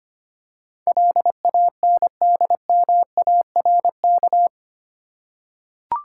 Morse code, heard as a keyed tone, about 25 wpm, 700 Hz. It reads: LANDMARK
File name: Eight Letter Words - 25wpm